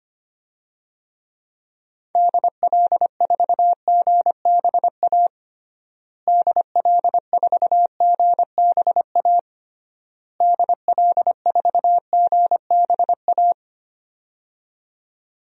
Morse code, heard as a keyed tone, about 25 wpm, 700 Hz. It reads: DL4GBA DL4GBA DL4GBA
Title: DXCC-100 Call Signs - 6 Characters - Repeated 3x - 25wpm